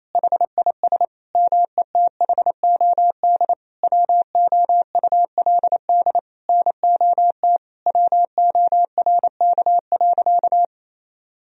Morse code, heard as a keyed tone, about 28 wpm, 700 Hz. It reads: HIS METHOD WOULD NOT WORK.
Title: Sentences from Top 1000 Words - Review Part 3 - 28wpm